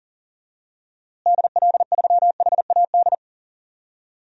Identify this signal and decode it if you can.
Morse code, heard as a keyed tone, about 40 wpm, 700 Hz. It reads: DL3HAD
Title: DXCC-100 Call Signs - 6 Characters - Repeated 2x - 40wpm